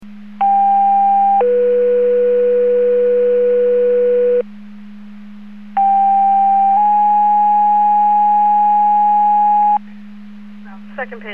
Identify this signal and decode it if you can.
Two-tone page, heard as a one-second tone, then a three-second tone, about 800 then 490 Hz, about 790 then 830 Hz. It is heard from 0.4 to 9.8 s.